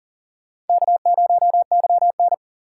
Morse code, heard as a keyed tone, about 52 wpm, 700 Hz. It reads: K0YN